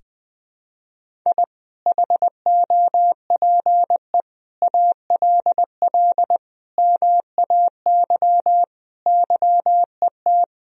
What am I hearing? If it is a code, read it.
Morse code, heard as a keyed tone, about 20 wpm, 700 Hz. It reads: I HOPE ALL MAY YET
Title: Sentences from Top 1000 Words - Review Part 1 - 20wpm